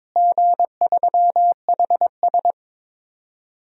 Morse code, heard as a keyed tone, about 22 wpm, 700 Hz. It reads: G3HS
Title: DXCC-100 Call Signs - 4 Characters - Repeated 3x - 22wpm